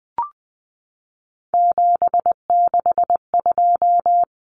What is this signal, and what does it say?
Morse code, heard as a keyed tone, about 20 wpm, 700 Hz. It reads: 762
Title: Common Cut Numbers - 3-Digit 20wpm